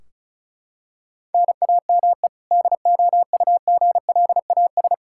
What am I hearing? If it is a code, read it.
Morse code, heard as a keyed tone, about 36 wpm, 700 Hz. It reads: NAME DOUGLAS
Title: Practice QSO - Exchange 1 - Beginning - 35wpm